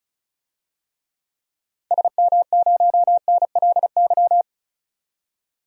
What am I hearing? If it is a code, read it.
Morse code, heard as a keyed tone, about 36 wpm, 700 Hz. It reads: SM0NLY
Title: DXCC-100 Call Signs - 6 Characters - 35wpm